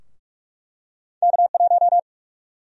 Morse code, heard as a keyed tone, about 45 wpm, 700 Hz. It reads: K1
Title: USA Call Sign Prefixes - Repeated 2x - 45wpm